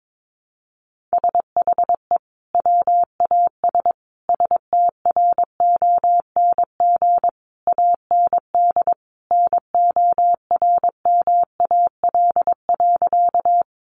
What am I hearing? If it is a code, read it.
Morse code, heard as a keyed tone, about 22 wpm, 700 Hz. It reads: SHE WAS STRONG AND NORMAL.
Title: Sentences from Top 1200 Words 22wpm